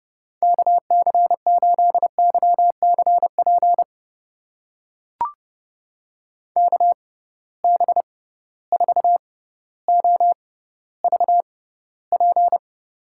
Morse code, heard as a keyed tone, about 30 wpm, spaced as for 15 wpm, 700 Hz. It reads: KC8YCP KB4OVP
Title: USA Call Signs - 6 Characters - 6x Character Spacing - 30wpm